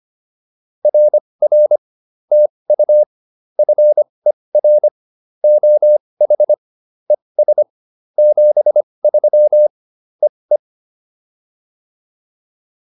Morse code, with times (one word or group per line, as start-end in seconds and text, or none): 0.8-1.8 RR
2.3-3.0 TU
3.6-4.9 FER
5.4-6.5 OH
7.1-7.6 ES
8.2-9.7 73
10.2-10.6 EE